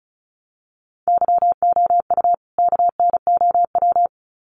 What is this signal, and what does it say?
Morse code, heard as a keyed tone, about 34 wpm, 700 Hz. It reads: YOU KNOW
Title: Sets of 2 Words 35wpm